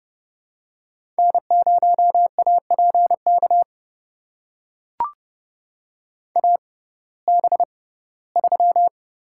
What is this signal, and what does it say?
Morse code, heard as a keyed tone, about 32 wpm, spaced as for 19 wpm, 700 Hz. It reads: N0APK AB3
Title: USA Call Signs - 5 Characters - 6x Character Spacing - 30wpm